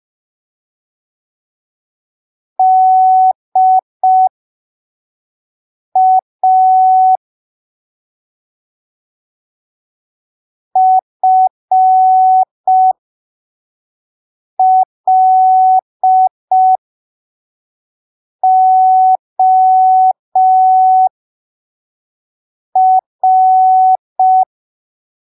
Morse code, heard as a keyed tone, about 5 wpm, spaced as for 3 wpm, 750 Hz. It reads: DA FLOR
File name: arquivos / texto / texto 20 05ppm